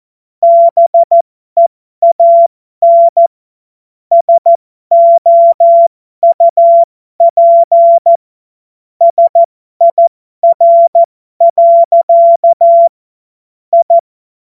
Morse code, heard as a keyed tone, about 14 wpm, spaced as for 12 wpm, 680 Hz. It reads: BEAN SOUP SIR. I
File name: Morsekurs / Witze